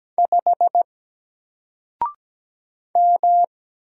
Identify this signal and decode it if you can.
Morse code, heard as a keyed tone, about 17 wpm, 700 Hz. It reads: 5 M
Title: Single Character (Letter:Number:Punctuation:Prosign) 17wpm